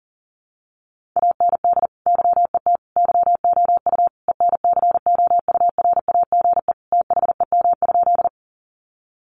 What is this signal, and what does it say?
Morse code, heard as a keyed tone, about 40 wpm, 700 Hz. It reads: AND YET YOU ENCOURAGE THEM?
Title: Sentences from Top 1400 Words 40wpm